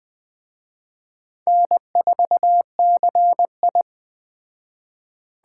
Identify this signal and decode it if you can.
Morse code, heard as a keyed tone, about 20 wpm, 700 Hz. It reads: N4CI
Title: Speed Racing - USA 1x2 Call Signs 17wpm